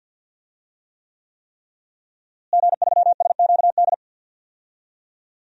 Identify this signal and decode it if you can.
Morse code, heard as a keyed tone, about 52 wpm, 700 Hz. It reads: G3SQD
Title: DXCC-100 Call Signs - 5 Characters - Repeated 3x - 50wpm